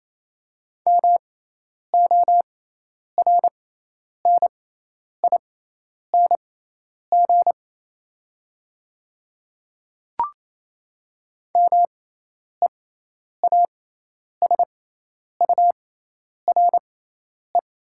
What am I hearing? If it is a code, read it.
Morse code, heard as a keyed tone, about 28 wpm, spaced as for 10 wpm, 700 Hz. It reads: MORNING MEASURE
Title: 6x Character Spacing - Top 500 Words - 28wpm